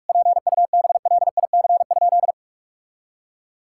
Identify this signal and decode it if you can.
Morse code, heard as a keyed tone, about 46 wpm, 690 Hz. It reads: PUBLIC?
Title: Sentences from Top 1000 Words - Review Part 5 - 45wpm